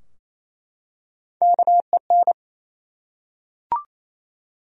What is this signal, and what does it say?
Morse code, heard as a keyed tone, about 28 wpm, 700 Hz. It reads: KEN
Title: USA Common Names - Max 4 Letters - 28wpm